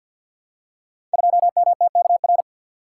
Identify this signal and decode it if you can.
Morse code, heard as a keyed tone, about 53 wpm, 700 Hz. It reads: 2MTKR
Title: Code Groups - 5 Letter:Numbers - 50wpm